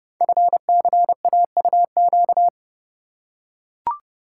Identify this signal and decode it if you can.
Morse code, heard as a keyed tone, about 30 wpm, 700 Hz. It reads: FCAUQ
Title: Code Groups - 5 Letters - 30wpm